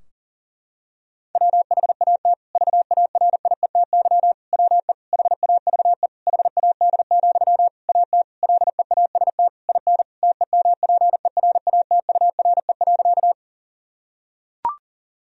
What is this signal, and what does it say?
Morse code, heard as a keyed tone, about 37 wpm, 700 Hz. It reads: WHAT VARIETY WE HAVE HAD, AT LEAST IN TEMPERATURE.
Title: Sentences from Top 1000 Words - Review Part 4 - 40wpm